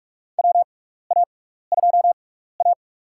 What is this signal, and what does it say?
Morse code, heard as a keyed tone, about 43 wpm, spaced as for 15 wpm, 700 Hz. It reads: WA2A
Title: USA Call Signs - 4 Characters - 6x Character Spacing - 45wpm